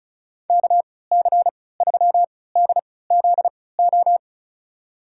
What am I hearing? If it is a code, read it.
Morse code, heard as a keyed tone, about 36 wpm, spaced as for 20 wpm, 690 Hz. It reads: KC3DZO
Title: USA Call Signs - 3x Character Spacing - 35wpm